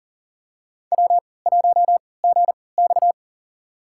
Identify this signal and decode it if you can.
Morse code, heard as a keyed tone, about 40 wpm, spaced as for 23 wpm, 700 Hz. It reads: W1GX
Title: USA Call Signs - 4 Characters - 3x Character Spacing - 40wpm